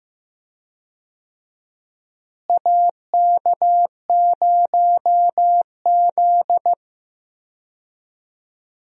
Morse code, heard as a keyed tone, about 15 wpm, 700 Hz. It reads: AK0Z
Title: USA Call Signs - 4 Characters - Repeated 2x - 15wpm